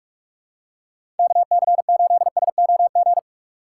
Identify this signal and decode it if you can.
Morse code, heard as a keyed tone, about 46 wpm, 700 Hz. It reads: KC8SOG
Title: USA Call Signs - 6 Characters - 4x Character Spacing - 45wpm